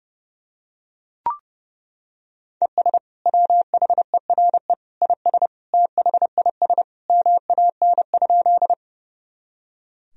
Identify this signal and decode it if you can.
Morse code, heard as a keyed tone, about 30 wpm, 700 Hz. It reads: ES WHERE IS THIS MAN?
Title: Sentences from Top 200 Words and Abbreviations 30wpm